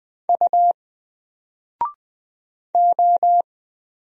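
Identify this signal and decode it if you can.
Morse code, heard as a keyed tone, about 20 wpm, 700 Hz. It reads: U O
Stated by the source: Single letter and number Morse drill at 20 wpm, from N4LCD.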